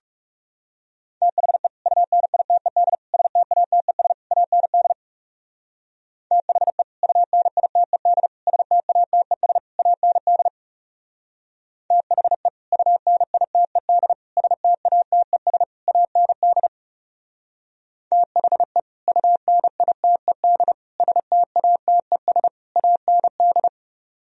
Morse code, heard as a keyed tone, about 36 wpm, 700 Hz. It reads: THE UNITED STATES AND THE UNITED STATES AND THE UNITED STATES AND THE UNITED STATES AND
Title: Speed-Racing - Sets of 4 Words 30wpm